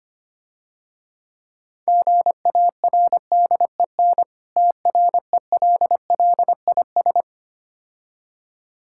Morse code, heard as a keyed tone, about 25 wpm, 700 Hz. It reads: GARDEN TRELLIS
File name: Difficult Words - Speed-Racing with Hints - 25wpm